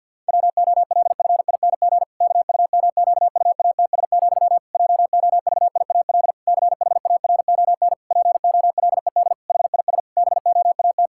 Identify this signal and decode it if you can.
Morse code, heard as a keyed tone, about 50 wpm, 700 Hz. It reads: WOLFING KUMQUATS, JOVIAL CHARON POLED HIS BOAT
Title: Pangram Challenge v2 - 50wpm